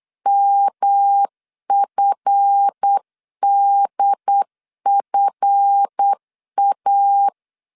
Morse code, heard as a keyed tone, about 8 wpm, 800 Hz. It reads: MFDFA
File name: LEZIONI / 7 GF9Z / 7 GF 8wpm 3108